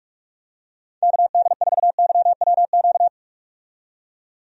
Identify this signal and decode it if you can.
Morse code, heard as a keyed tone, about 46 wpm, 700 Hz. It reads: KD4YWQ
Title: USA Call Signs - 6 Characters - 45wpm